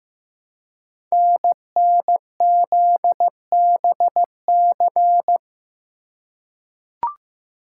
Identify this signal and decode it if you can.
Morse code, heard as a keyed tone, about 15 wpm, 700 Hz. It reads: NNZBC
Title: Code Groups - 5 Letters - 15wpm